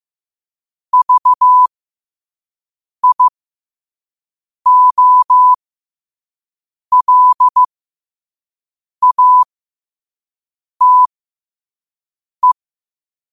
Morse code, heard as a keyed tone, about 15 wpm, spaced as for 5 wpm, 1000 Hz. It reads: VIOLATE
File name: qotdfiles / 5wpm